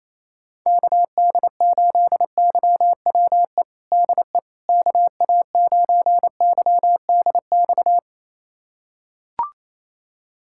Morse code, heard as a keyed tone, about 28 wpm, 700 Hz. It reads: KD8YWE DE KA9YDX
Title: USA Callsign DE Callsign 28wpm